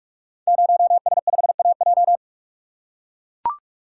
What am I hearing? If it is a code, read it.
Morse code, heard as a keyed tone, about 44 wpm, 690 Hz. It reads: 0S5AJ
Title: Code Groups - 5 Letter:Numbers - 45wpm